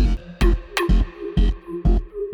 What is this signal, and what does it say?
There is no code.